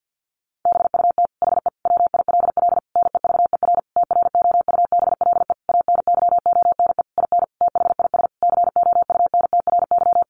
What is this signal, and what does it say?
Morse code, heard as a keyed tone, about 50 wpm, 700 Hz. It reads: BUT HE WILL NEVER TROUBLE ANYONE IN THIS COUNTRY
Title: Sentences from Top 1200 Words 50wpm